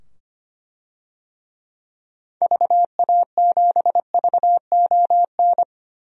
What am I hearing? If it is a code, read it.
Morse code, heard as a keyed tone, about 25 wpm, 700 Hz. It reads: VA7VON
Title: DXCC-100 Call Signs - 6 Characters - 25wpm